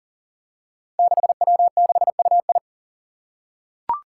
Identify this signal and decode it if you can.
Morse code, heard as a keyed tone, about 40 wpm, 700 Hz. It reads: 6W6UI
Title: Five Challenging Letter-Number Combinations 40wpm